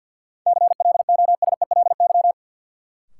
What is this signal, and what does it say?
Morse code, heard as a keyed tone, about 53 wpm, 700 Hz. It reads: CLOSELY
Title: Sentences from Top 1400 Words - New Word Drill - 50wpm